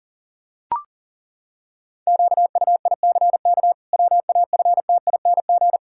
Morse code, heard as a keyed tone, about 40 wpm, 700 Hz. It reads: QUICK WAFTING
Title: Pangram Challenge v2 - 40wpm